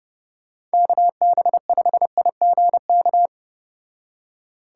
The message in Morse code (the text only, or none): KB5IGK